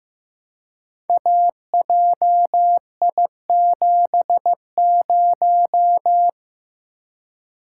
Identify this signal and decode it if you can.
Morse code, heard as a keyed tone, about 15 wpm, 700 Hz. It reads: AJI70
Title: Code Groups - 5 Letter:Numbers - 15wpm